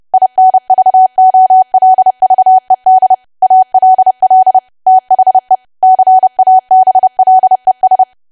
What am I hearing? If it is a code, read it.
Morse code, heard as a keyed tone, about 30 wpm, 750 Hz. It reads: INVOLVED ALL THE CABLES
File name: cwa / W1AW-Level3 / 30WPM